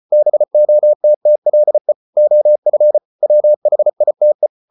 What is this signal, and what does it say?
Morse code, heard as a keyed tone, about 34 wpm, 600 Hz. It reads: BOTTLE OF WHITE